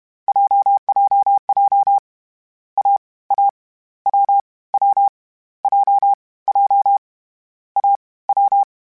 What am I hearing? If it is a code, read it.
Morse code, heard as a keyed tone, about 31 wpm, 800 Hz. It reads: JJJ A A W W J J A W
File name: Difficult Combos / Difficult Combos-mp3s / AWJ @30